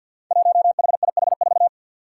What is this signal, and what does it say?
Morse code, heard as a keyed tone, about 52 wpm, 700 Hz. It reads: 1HIH4
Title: Five Challenging Letter-Number Combinations 50wpm